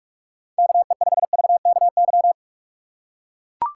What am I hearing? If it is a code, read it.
Morse code, heard as a keyed tone, about 43 wpm, 700 Hz. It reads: KE5VKY